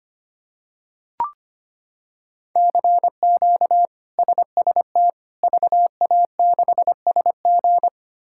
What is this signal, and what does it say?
Morse code, heard as a keyed tone, about 25 wpm, 700 Hz. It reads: CQ SST VA6SG
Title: K1USN SST QSO Practice - 25wpm